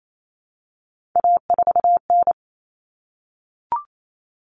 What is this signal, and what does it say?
Morse code, heard as a keyed tone, about 28 wpm, 700 Hz. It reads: A4N
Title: Common Cut Numbers - 3-Digit 28wpm